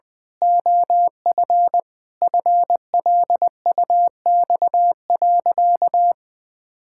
Morse code, heard as a keyed tone, about 20 wpm, 700 Hz. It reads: OF FLUX.
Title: Pangram Challenge v2 - 20wpm